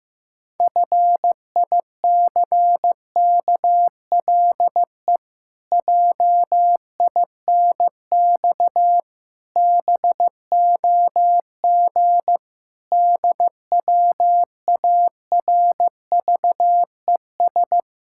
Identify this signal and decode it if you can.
Morse code, heard as a keyed tone, about 15 wpm, 700 Hz. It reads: FICKLE JINX BOG DWARVES
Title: Pangram Challenge v2 - 15wpm